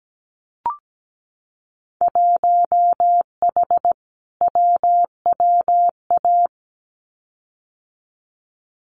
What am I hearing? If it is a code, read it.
Morse code, heard as a keyed tone, about 17 wpm, 700 Hz. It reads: 1H WWA
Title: Winter Field Day Exchanges 17wpm